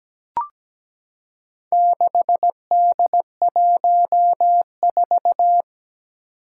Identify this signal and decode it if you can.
Morse code, heard as a keyed tone, about 17 wpm, 700 Hz. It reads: 6D14
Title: Four Challenging Letter-Number Combinations 17wpm